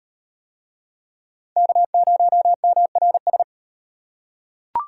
Morse code, heard as a keyed tone, about 38 wpm, 700 Hz. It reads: K0MRS